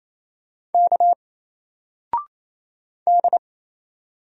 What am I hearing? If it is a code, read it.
Morse code, heard as a keyed tone, about 28 wpm, 700 Hz. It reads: K D
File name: Single-Letter-Number-28wpm